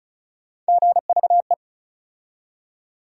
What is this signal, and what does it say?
Morse code, heard as a keyed tone, about 36 wpm, 700 Hz. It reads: GVE
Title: USA Call Sign Postfixes - 3 Letters 35wpm